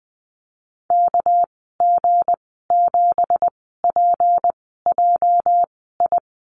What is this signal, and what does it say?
Morse code, heard as a keyed tone, about 20 wpm, spaced as for 15 wpm, 700 Hz. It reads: KG7PJI
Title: 219 Lesson 35g - Number 0 - Call Signs 20wpm